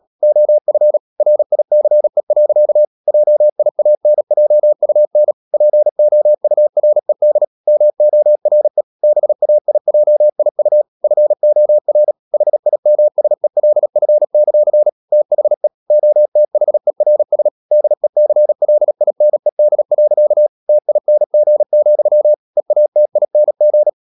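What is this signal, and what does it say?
Morse code, heard as a keyed tone, about 37 wpm, 600 Hz. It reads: OF RICE. JIANJUN POURED MORE BAIJIU FOR HIMSELF; THE OTHERS DECLINED. TING, EATING
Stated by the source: Morse code practice at 35–39 wpm sending English text of Ken Follett's novel Never, from CW med Gnister.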